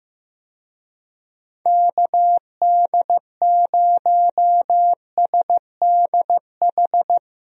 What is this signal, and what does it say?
Morse code, heard as a keyed tone, about 15 wpm, 700 Hz. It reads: KD0SDH
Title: USA Call Signs - 6 Characters - Repeated 3x - 15wpm